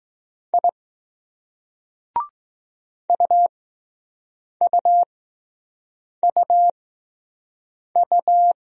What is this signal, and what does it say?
Morse code, heard as a keyed tone, about 19 wpm, spaced as for 6 wpm, 700 Hz. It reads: I UUUU